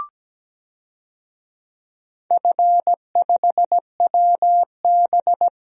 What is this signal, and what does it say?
Morse code, heard as a keyed tone, about 17 wpm, 700 Hz. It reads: F5WB
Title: DXCC-100 Call Signs - 4 Characters - Repeated 3x - 17wpm